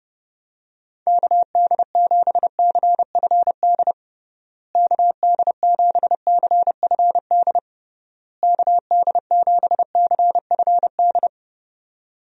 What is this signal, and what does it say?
Morse code, heard as a keyed tone, about 32 wpm, 700 Hz. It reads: KD7CFD KD7CFD KD7CFD
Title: USA Call Signs - 6 Characters - Repeated 3x - 30wpm